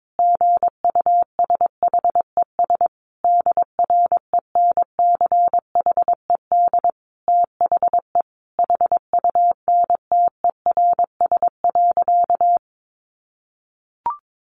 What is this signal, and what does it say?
Morse code, heard as a keyed tone, about 22 wpm, 700 Hz. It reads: GUSHES DRENCHED THE HUNTERS.
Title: A Sound Of Thunder 22wpm